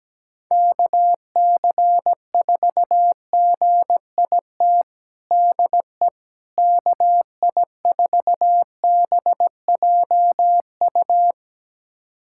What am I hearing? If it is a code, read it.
Morse code, heard as a keyed tone, about 17 wpm, 700 Hz. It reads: KC4GIT DE KI4BJU
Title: USA Callsign DE Callsign 17wpm